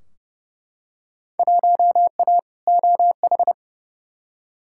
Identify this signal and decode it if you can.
Morse code, heard as a keyed tone, about 30 wpm, 700 Hz. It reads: 1A OH
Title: Field Day Exchanges - 30wpm